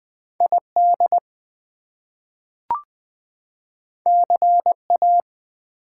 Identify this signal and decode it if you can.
Morse code, heard as a keyed tone, about 20 wpm, 700 Hz. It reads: ID CA
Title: US States and Canadian Provinces 20wpm